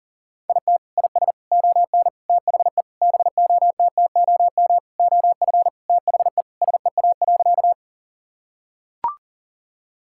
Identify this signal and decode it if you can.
Morse code, heard as a keyed tone, about 40 wpm, 700 Hz. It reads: IT IS ON THE BOTTOM OF THE SEA.
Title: Sentences from Top 1000 Words - Review Part 6 - 40wpm